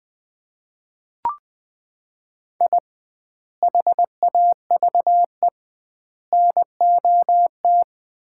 Morse code, heard as a keyed tone, about 20 wpm, 700 Hz. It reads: I HAVE NOT
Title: 2x Word-Spacing - Sets of 3 Words - 20wpm